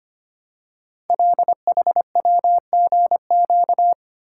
Morse code, heard as a keyed tone, about 25 wpm, 700 Hz. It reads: LHWGQ
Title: Code Groups - 5 Letters - 25wpm